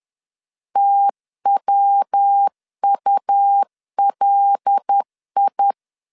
Morse code, heard as a keyed tone, about 10 wpm, 800 Hz. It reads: TWULI